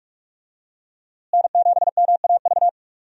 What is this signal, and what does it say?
Morse code, heard as a keyed tone, about 42 wpm, 700 Hz. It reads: N7MAV
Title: USA Call Signs - 5 Characters - Repeated 3x - 45wpm